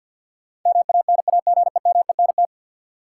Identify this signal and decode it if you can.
Morse code, heard as a keyed tone, about 50 wpm, 700 Hz. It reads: MANAGEMENT
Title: Sentences from Top 1000 Words - Review Part 4 - New Word Drill - 50wpm